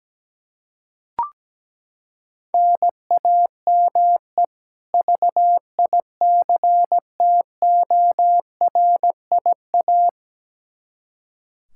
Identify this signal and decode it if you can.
Morse code, heard as a keyed tone, about 17 wpm, 700 Hz. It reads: NAME VICTORIA
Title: Practice QSO - Exchange 1 - Beginning - 17wpm